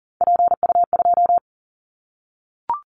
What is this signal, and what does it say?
Morse code, heard as a keyed tone, about 40 wpm, 700 Hz. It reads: PU2
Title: Callsign Prefixes for Top 100 DXCC - 40wpm